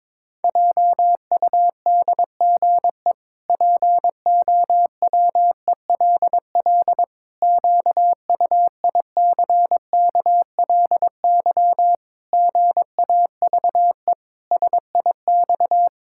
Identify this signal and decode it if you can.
Morse code, heard as a keyed tone, about 22 wpm, 700 Hz. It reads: JUDGE POWELL QUICKLY GAVE SIX
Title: Pangram Challenge v2 - 22wpm